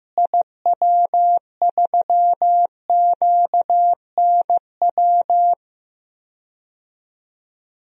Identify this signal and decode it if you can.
Morse code, heard as a keyed tone, about 15 wpm, 700 Hz. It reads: IW3QNW
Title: DXCC-100 Call Signs - 6 Characters - Repeated 3x - 15wpm